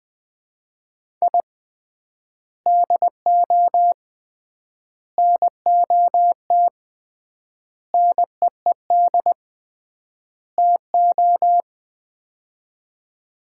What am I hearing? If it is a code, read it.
Morse code, heard as a keyed tone, about 20 wpm, 700 Hz. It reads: I DO NOT NEED TO
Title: 3x Word-Spacing - Sets of 5 Words - 20wpm